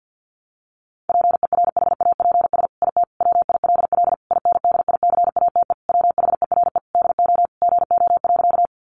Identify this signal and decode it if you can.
Morse code, heard as a keyed tone, about 50 wpm, 700 Hz. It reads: PERHAPS IT WILL INDICATE WHERE DO GO.
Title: Sentences from Top 1300 Words 50wpm